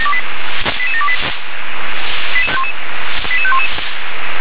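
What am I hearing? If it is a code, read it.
Five-tone selective call (ZVEI), heard as quick runs of five tones, tones of about 70 ms. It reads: 88728 88511
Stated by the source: ZVEI five-tone call received with an SDR.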